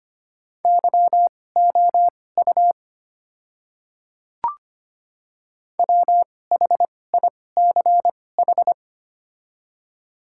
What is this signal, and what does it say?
Morse code, heard as a keyed tone, about 25 wpm, spaced as for 18 wpm, 700 Hz. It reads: YOU WHICH
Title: 2x Character Spacing - Top 100 Words - 25wpm